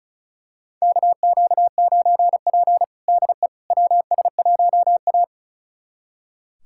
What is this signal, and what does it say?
Morse code, heard as a keyed tone, about 34 wpm, 700 Hz. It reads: KQ9P DE WS1A
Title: USA Callsign DE Callsign - 4 Characters - 35wpm